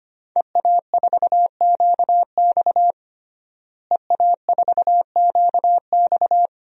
Morse code, heard as a keyed tone, about 25 wpm, 700 Hz. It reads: EA4QX EA4QX
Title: DXCC-100 Call Signs - 5 Characters - Repeated 3x - 25wpm